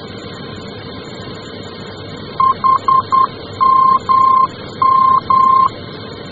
Morse code, heard as a keyed tone, about 10 wpm, 1060 Hz. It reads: HMM
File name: germany / mx / HMM-D